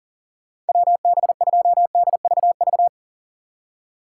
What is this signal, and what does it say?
Morse code, heard as a keyed tone, about 48 wpm, 700 Hz. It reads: WB2DVV
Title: Speed Racing - USA 2x3 Call Signs 40wpm